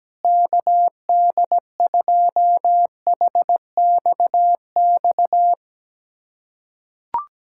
Morse code, heard as a keyed tone, about 17 wpm, 700 Hz. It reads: KD2HXX
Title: USA Callsign DE Callsign 17wpm